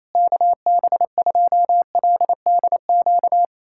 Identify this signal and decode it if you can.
Morse code, heard as a keyed tone, about 28 wpm, 700 Hz. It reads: KB2LDQ
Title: USA Callsign DE Callsign - 6 Characters - 28wpm